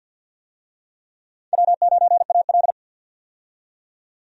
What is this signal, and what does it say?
Morse code, heard as a keyed tone, about 52 wpm, 700 Hz. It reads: W9AL
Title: USA Call Signs - 4 Characters - 50wpm